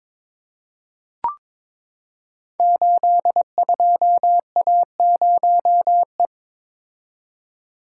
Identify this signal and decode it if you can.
Morse code, heard as a keyed tone, about 22 wpm, 690 Hz. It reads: 82A0E
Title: Code Groups - 5 Letter:Numbers - 22wpm